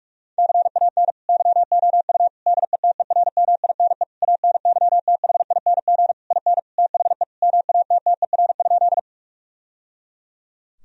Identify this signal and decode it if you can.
Morse code, heard as a keyed tone, about 46 wpm, 700 Hz. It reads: CAN YOU DETERMINE ANYTHING IN THE MATTER?
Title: Sentences from Top 1000 Words - Review Part 4 - 45wpm